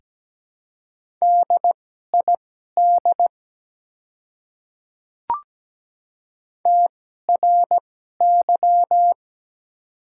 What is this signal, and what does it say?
Morse code, heard as a keyed tone, about 17 wpm, spaced as for 12 wpm, 700 Hz. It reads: DID TRY
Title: 2x Character Spacing - Top 200 Words - 17wpm